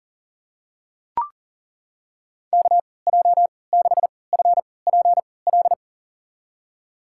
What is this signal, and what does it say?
Morse code, heard as a keyed tone, about 40 wpm, spaced as for 23 wpm, 700 Hz. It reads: KJ6FPL
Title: USA Call Signs - 6 Characters - 3x Character Spacing - 40wpm